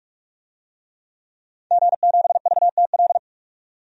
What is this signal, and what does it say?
Morse code, heard as a keyed tone, about 46 wpm, 700 Hz. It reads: G7VTL
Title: DXCC-100 Call Signs - 5 Characters - 45wpm